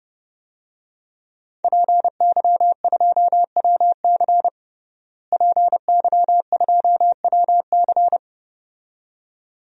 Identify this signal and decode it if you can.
Morse code, heard as a keyed tone, about 30 wpm, 700 Hz. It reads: PY2WC PY2WC
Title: DXCC-100 Call Signs - 5 Characters - Repeated 2x - 30wpm